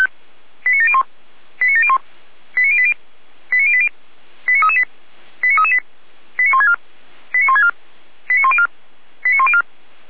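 Five-tone selective call (ZVEI), heard as quick runs of five tones, tones of about 70 ms. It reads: 89701 89701 89080 89080 89338 89338 89164 89164 89104 89104
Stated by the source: Received by SDR as a ZVEI five-tone call.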